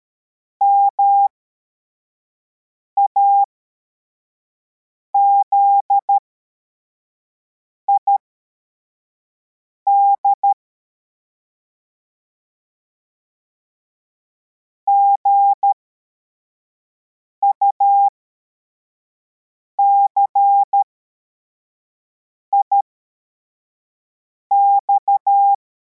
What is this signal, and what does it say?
Morse code, heard as a keyed tone, about 13 wpm, spaced as for 4 wpm, 800 Hz. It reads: MAZID GUCIX
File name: formation / cw / Text41